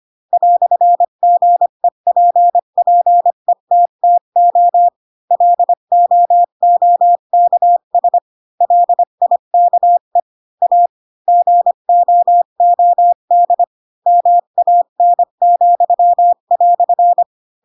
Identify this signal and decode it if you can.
Morse code, heard as a keyed tone, about 25 wpm, 700 Hz. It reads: "GEPPETTO LOOKS LIKE A GOOD MAN,"
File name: Pinocchio / Pinocchio ch3 20 25 ik1hge